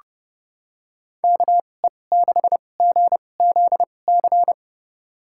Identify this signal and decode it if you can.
Morse code, heard as a keyed tone, about 30 wpm, spaced as for 22 wpm, 700 Hz. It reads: KE6GZC
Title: USA Call Signs - 6 Characters - 2x Character Spacing - 30wpm